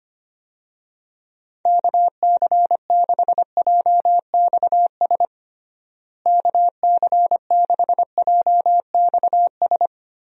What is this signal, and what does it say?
Morse code, heard as a keyed tone, about 25 wpm, 700 Hz. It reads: KC6JXS KC6JXS
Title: USA Call Signs - 6 Characters - Repeated 2x - 25wpm